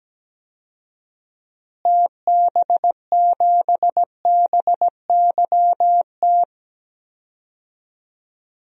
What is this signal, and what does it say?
Morse code, heard as a keyed tone, about 17 wpm, 700 Hz. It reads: TB7BYT
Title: DXCC-100 Call Signs - 6 Characters - Repeated 3x - 17wpm